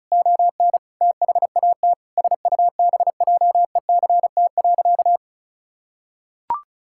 Morse code, heard as a keyed tone, about 34 wpm, 700 Hz. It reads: ON THAT SUBJECT.
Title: Sentences from Top 1600 Words 35wpm